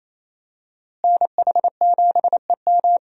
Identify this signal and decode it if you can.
Morse code, heard as a keyed tone, about 28 wpm, 700 Hz. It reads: NH7EM
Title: USA Call Signs - 5 Characters - Repeated 3x - 28wpm